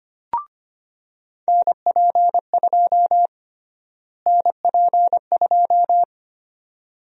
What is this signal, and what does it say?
Morse code, heard as a keyed tone, about 25 wpm, 700 Hz. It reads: NP2 NP2
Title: USA Call Sign Prefixes - Repeated 2x - 25wpm